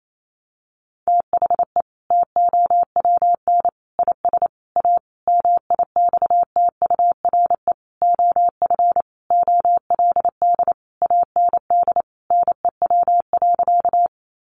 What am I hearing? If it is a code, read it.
Morse code, heard as a keyed tone, about 28 wpm, 700 Hz. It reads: THE TOWN IS A MIXTURE OF OLD AND NEW.
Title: Sentences from Top 1500 Words 28wpm